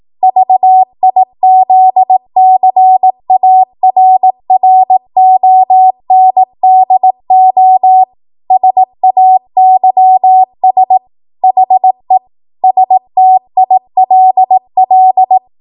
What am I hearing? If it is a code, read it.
Morse code, heard as a keyed tone, about 18 wpm, 750 Hz. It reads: VIZCARRONDO SAYS HE STILL
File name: Archive / 18 WPM / 18WPM